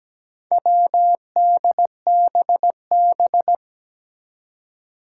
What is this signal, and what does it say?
Morse code, heard as a keyed tone, about 17 wpm, 700 Hz. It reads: WDBB
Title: Four Challenging Letter-Number Combinations 17wpm